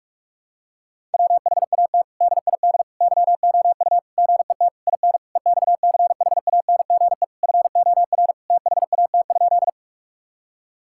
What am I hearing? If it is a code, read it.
Morse code, heard as a keyed tone, about 47 wpm, 700 Hz. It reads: WHAT DID YOU GET IN EXCHANGE FOR THAT?
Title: Sentences from Top 1000 Words - Review Part 6 - 45wpm